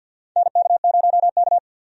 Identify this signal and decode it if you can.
Morse code, heard as a keyed tone, about 52 wpm, 700 Hz. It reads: NK0K
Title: USA Call Signs - 4 Characters - 50wpm